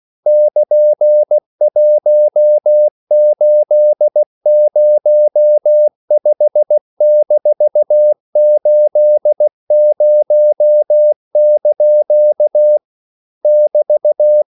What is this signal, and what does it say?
Morse code, heard as a keyed tone, about 16 wpm, 600 Hz. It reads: <KN>1805-80) <BT>
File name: CW / Practice / EnglishQuotations0216